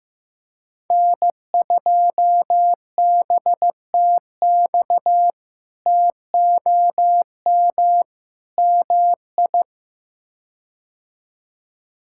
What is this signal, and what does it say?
Morse code, heard as a keyed tone, about 16 wpm, 700 Hz. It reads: N2BTX TOM MI
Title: K1USN SST QSO Practice - 15wpm